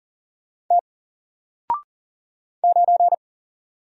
Morse code, heard as a keyed tone, about 40 wpm, 700 Hz. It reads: T 9
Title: Single Letter-Number Alt 40wpm